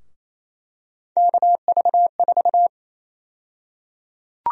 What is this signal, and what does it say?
Morse code, heard as a keyed tone, about 28 wpm, 700 Hz. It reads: KV4
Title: USA Call Sign Prefixes - 28wpm